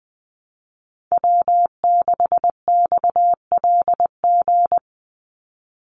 Morse code, heard as a keyed tone, about 20 wpm, 700 Hz. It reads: W6XLG
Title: USA Call Signs 20wpm